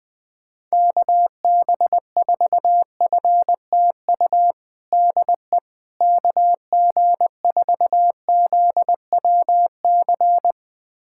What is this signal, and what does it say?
Morse code, heard as a keyed tone, about 19 wpm, 700 Hz. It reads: KB4FTU DE KG4ZWC
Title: USA Callsign DE Callsign - 6 Characters - 20wpm